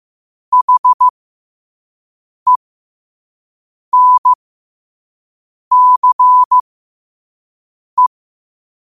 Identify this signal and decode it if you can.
Morse code, heard as a keyed tone, about 15 wpm, spaced as for 5 wpm, 1000 Hz. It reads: HENCE
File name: qotdfiles / 5wpm